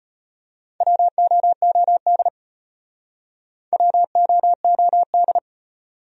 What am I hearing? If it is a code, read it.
Morse code, heard as a keyed tone, about 35 wpm, 700 Hz. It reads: WOOD WOOD